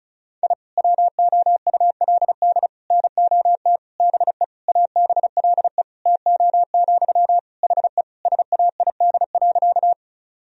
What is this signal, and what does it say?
Morse code, heard as a keyed tone, about 36 wpm, 700 Hz. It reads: I WOULD NOT BE ABLE TO, HE SAID.
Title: Sentences from Top 1000 Words - Review Part 2 - 35wpm